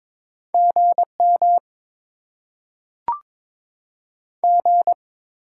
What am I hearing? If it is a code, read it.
Morse code, heard as a keyed tone, about 22 wpm, 700 Hz. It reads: G M G
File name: Top 20 QSO Elements - 8x Character Spacing - 22wpm